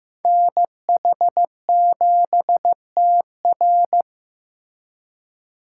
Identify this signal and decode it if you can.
Morse code, heard as a keyed tone, about 18 wpm, 700 Hz. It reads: NH7TR
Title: Speed Racing - USA 2x2 Call Signs 15wpm